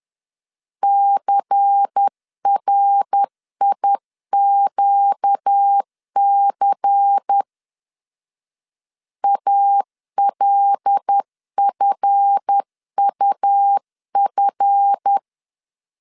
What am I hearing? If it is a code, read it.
Morse code, heard as a keyed tone, about 10 wpm, 800 Hz. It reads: CRIQC ALFUF
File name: LEZIONI / 7 GF9Z / 7 GF 10wpm 2514